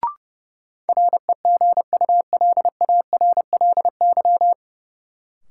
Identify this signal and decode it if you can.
Morse code, heard as a keyed tone, about 32 wpm, 700 Hz. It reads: REGULARLY
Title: Sentences from Top 1400 Words - New Word Drill - 30wpm